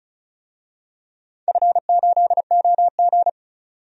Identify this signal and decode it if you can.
Morse code, heard as a keyed tone, about 34 wpm, 700 Hz. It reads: F8OG
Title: DXCC-100 Call Signs - 4 Characters - 35wpm